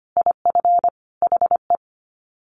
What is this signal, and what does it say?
Morse code, heard as a keyed tone, about 25 wpm, 700 Hz. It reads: IF HE
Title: Sets of 5 Words Encore2 25wpm